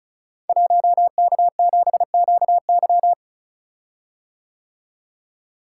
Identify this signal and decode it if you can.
Morse code, heard as a keyed tone, about 36 wpm, 700 Hz. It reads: 1K7QY
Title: Code Groups - 5 Letter:Numbers - 35wpm